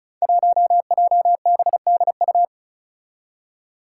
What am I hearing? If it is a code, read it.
Morse code, heard as a keyed tone, about 36 wpm, 700 Hz. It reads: 1JBDU
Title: Five Challenging Letter-Number Combinations 35wpm